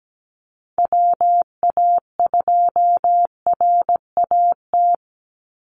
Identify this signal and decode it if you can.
Morse code, heard as a keyed tone, about 17 wpm, 700 Hz. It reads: WA2RAT
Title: USA Call Signs 17wpm